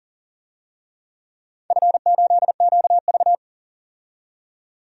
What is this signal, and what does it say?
Morse code, heard as a keyed tone, about 38 wpm, 700 Hz. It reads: F8QV
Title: DXCC-100 Call Signs - 4 Characters - 40wpm